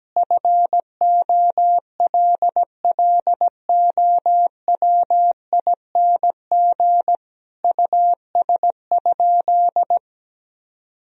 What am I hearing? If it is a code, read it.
Morse code, heard as a keyed tone, about 17 wpm, 700 Hz. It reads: FOLLOWING US?
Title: Sentences from Top 1000 Words - Review Part 4 - 17wpm